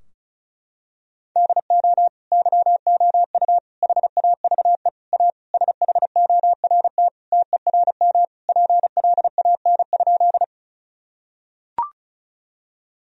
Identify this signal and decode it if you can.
Morse code, heard as a keyed tone, about 34 wpm, 700 Hz. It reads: DO YOU HAVE A SHORT TERM PLAN?
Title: Sentences from Top 1000 Words - Review Part 4 - 35wpm